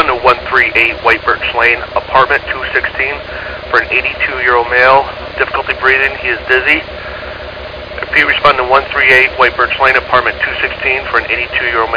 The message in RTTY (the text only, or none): none